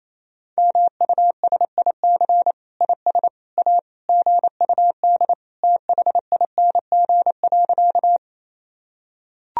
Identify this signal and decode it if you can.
Morse code, heard as a keyed tone, about 28 wpm, 700 Hz. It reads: MUSIC IS A GUD THING.
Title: Sentences from Top 300 Words and Abbreviations 28wpm